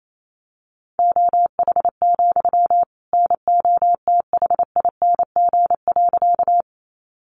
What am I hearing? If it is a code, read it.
Morse code, heard as a keyed tone, about 28 wpm, 700 Hz. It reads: OH, NOTHING.
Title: Sentences from Top 500 Words 28wpm